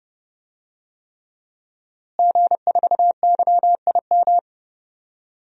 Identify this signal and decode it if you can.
Morse code, heard as a keyed tone, about 31 wpm, 700 Hz. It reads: G4YIM
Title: DXCC-100 Call Signs - 5 Characters - Repeated 3x - 30wpm